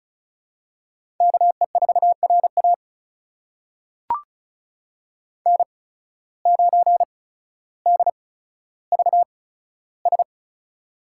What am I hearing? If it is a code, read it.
Morse code, heard as a keyed tone, about 34 wpm, spaced as for 15 wpm, 700 Hz. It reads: KE4RA N9DVS
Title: USA Call Signs - 5 Characters - 8x Character Spacing - 35wpm